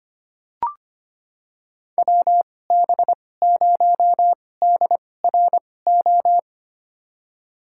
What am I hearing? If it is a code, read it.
Morse code, heard as a keyed tone, about 25 wpm, spaced as for 18 wpm, 700 Hz. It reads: WB0DRO
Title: USA Call Signs - 2x Character Spacing - 25wpm